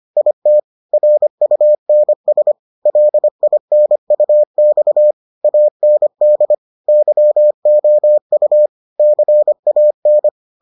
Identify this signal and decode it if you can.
Morse code, heard as a keyed tone, about 25 wpm, 600 Hz. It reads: IT RUNS LINUX AND YOU CAN